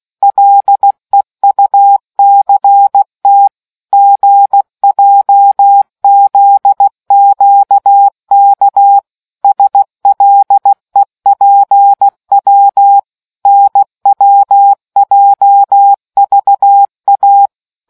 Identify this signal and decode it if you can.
Morse code, heard as a keyed tone, about 16 wpm, 800 Hz. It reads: LEUCT GJZQK SLEPW NWJVA